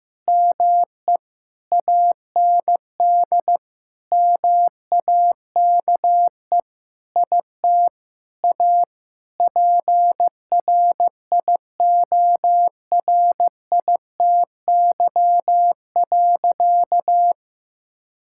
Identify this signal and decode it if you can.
Morse code, heard as a keyed tone, about 15 wpm, 700 Hz. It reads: ME AND MAKE IT A PRIORITY.
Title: Sentences from Top 1600 Words 15wpm